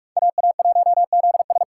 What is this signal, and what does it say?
Morse code, heard as a keyed tone, about 42 wpm, 700 Hz. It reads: AA1ZS